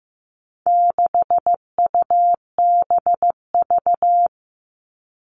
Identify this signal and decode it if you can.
Morse code, heard as a keyed tone, about 15 wpm, 700 Hz. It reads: 6UBV
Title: All Cut Numbers - 4-Digit 15wpm